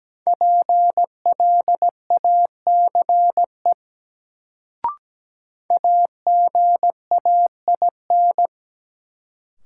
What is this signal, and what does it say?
Morse code, heard as a keyed tone, about 17 wpm, 700 Hz. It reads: PLACE AGAIN
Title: Sets of 4 Words - New Word Drill - 17wpm